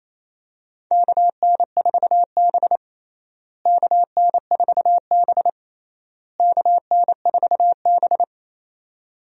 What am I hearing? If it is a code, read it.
Morse code, heard as a keyed tone, about 28 wpm, 700 Hz. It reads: KN4B KN4B KN4B
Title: USA Call Signs - 4 Characters - Repeated 3x - 28wpm